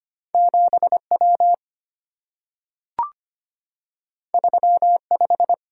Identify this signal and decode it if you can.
Morse code, heard as a keyed tone, about 25 wpm, 700 Hz. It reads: 7W 35
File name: Two Letter-Number Combinations - 25wpm